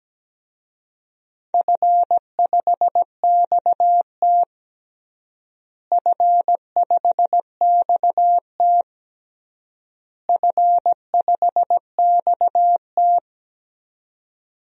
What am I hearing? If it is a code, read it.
Morse code, heard as a keyed tone, about 17 wpm, 700 Hz. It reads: F5XT F5XT F5XT
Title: DXCC-100 Call Signs - 4 Characters - Repeated 3x - 17wpm